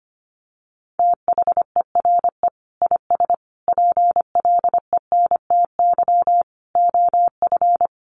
Morse code, heard as a keyed tone, about 25 wpm, 700 Hz. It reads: THERE IS PLENTY OF
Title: Sentences from Top 1300 Words 25wpm